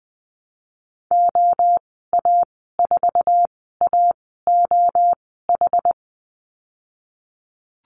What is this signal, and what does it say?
Morse code, heard as a keyed tone, about 20 wpm, spaced as for 15 wpm, 700 Hz. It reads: OA4AOH
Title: 076 Lesson 13g - Letter L - Call Signs 20wpm